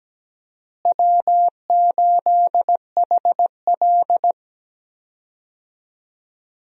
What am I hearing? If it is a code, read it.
Morse code, heard as a keyed tone, about 17 wpm, 700 Hz. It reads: W8HL
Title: USA Call Signs - 4 Characters - Repeated 2x - 17wpm